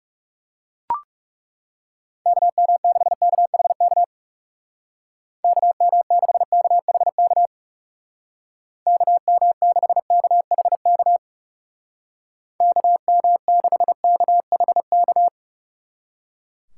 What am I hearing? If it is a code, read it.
Morse code, heard as a keyed tone, about 37 wpm, 700 Hz. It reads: KM6KHK KM6KHK KM6KHK KM6KHK
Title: Speed Racing - USA 2x3 Call Signs 30wpm